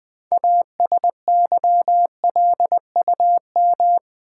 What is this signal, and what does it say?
Morse code, heard as a keyed tone, about 20 wpm, 700 Hz. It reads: ASYLUM
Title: 6 Letter Words - First 3 Letters Given - 20wpm